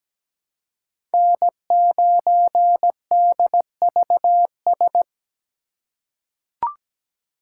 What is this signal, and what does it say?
Morse code, heard as a keyed tone, about 17 wpm, 700 Hz. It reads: N9DVS
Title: USA Call Signs - 5 Characters - 2x Character Spacing - 17wpm